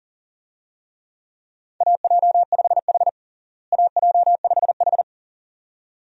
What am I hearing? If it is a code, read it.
Morse code, heard as a keyed tone, about 37 wpm, 700 Hz. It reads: AJ5H AJ5H
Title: USA Call Signs - 4 Characters - Repeated 2x - 40wpm